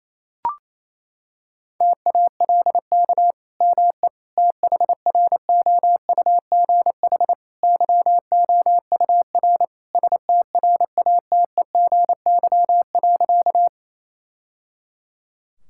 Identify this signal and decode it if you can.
Morse code, heard as a keyed tone, about 27 wpm, 700 Hz. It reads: TALK ME THROUGH YOUR STRATEGY.
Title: Sentences from Top 1000 Words - Review Part 5 - 28wpm